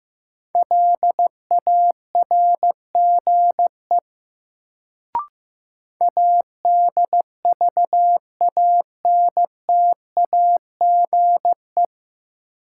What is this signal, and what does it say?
Morse code, heard as a keyed tone, about 15 wpm, 700 Hz. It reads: LARGE ADVANTAGE
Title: Sets of 4 Words - New Word Drill - 15wpm